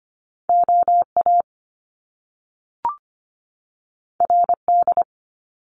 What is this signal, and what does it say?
Morse code, heard as a keyed tone, about 25 wpm, 700 Hz. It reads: OA RD
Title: Top Two Letter Combinations in Words 25wpm